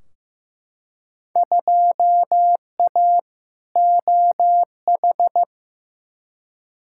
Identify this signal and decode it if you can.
Morse code, heard as a keyed tone, about 15 wpm, 700 Hz. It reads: 2A OH
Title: Field Day Exchanges - 15wpm